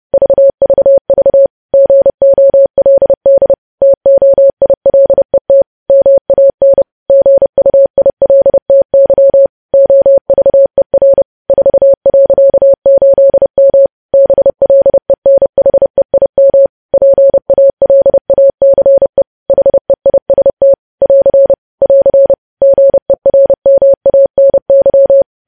Morse code, heard as a keyed tone, about 30 wpm, 550 Hz. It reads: VVV GOLD TOILET MAN GUILTY OVER 4.8M BLENHEIM PALACE HEIST <AR> <AR> GERMANY
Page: news headlines in morse code at 30 wpm